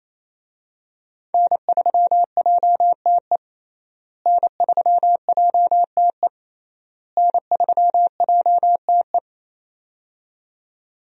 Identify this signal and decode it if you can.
Morse code, heard as a keyed tone, about 28 wpm, 700 Hz. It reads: N3JTE N3JTE N3JTE
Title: USA Call Signs - 5 Characters - Repeated 3x - 28wpm